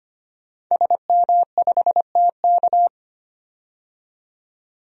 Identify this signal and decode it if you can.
Morse code, heard as a keyed tone, about 25 wpm, 700 Hz. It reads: SM5TK
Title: DXCC-100 Call Signs - 5 Characters - 25wpm